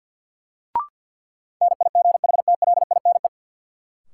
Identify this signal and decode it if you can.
Morse code, heard as a keyed tone, about 54 wpm, 700 Hz. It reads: NIGHTLINE
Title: Sets of 5 Words - New Word Drill - 50wpm